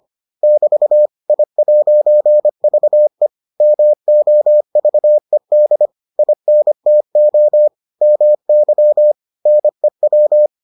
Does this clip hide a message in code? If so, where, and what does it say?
Yes: Morse code from 0.4 to 10.5 s, heard as a keyed tone, about 25 wpm, 600 Hz. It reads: <BT> I'VE MOVED INTO MY NEW